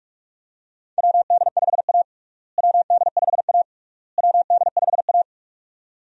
Morse code, heard as a keyed tone, about 46 wpm, 700 Hz. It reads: WD5A WD5A WD5A
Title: USA Call Signs - 4 Characters - Repeated 3x - 45wpm